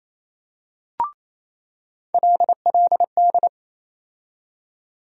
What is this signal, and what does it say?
Morse code, heard as a keyed tone, about 28 wpm, 700 Hz. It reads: LLD
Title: USA Call Sign Postfixes - 3 Letters 28wpm